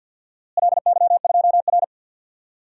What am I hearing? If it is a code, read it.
Morse code, heard as a keyed tone, about 48 wpm, 700 Hz. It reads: LY2R
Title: DXCC-100 Call Signs - 4 Characters - Repeated 2x - 50wpm